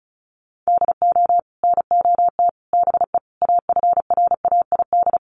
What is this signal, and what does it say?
Morse code, heard as a keyed tone, about 34 wpm, 700 Hz. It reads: DO NOT BE AFRAID
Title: Sets of 5 Words Encore2 35wpm